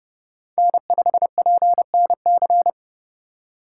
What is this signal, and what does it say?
Morse code, heard as a keyed tone, about 30 wpm, 700 Hz. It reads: N5PNC
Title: USA Call Signs - 3x Character Spacing - 30wpm